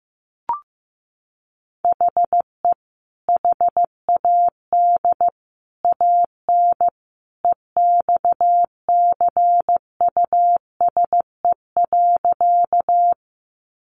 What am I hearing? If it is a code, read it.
Morse code, heard as a keyed tone, about 15 wpm, 700 Hz. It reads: HE HAD AN EXCUSE.
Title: Sentences from Top 800 Words 15wpm